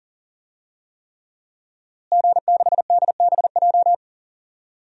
Morse code, heard as a keyed tone, about 40 wpm, 700 Hz. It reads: G6DBJ
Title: DXCC-100 Call Signs - 5 Characters - Repeated 3x - 40wpm